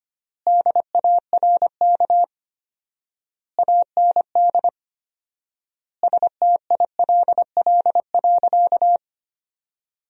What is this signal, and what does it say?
Morse code, heard as a keyed tone, about 25 wpm, 700 Hz. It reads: DARK AND STILL.
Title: 4x Word-Spacing - Sentences from Top 500 Words - 25wpm